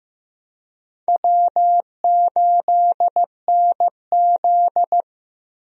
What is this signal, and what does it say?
Morse code, heard as a keyed tone, about 16 wpm, 700 Hz. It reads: W8NZ
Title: USA Call Signs - 4 Characters - Repeated 3x - 15wpm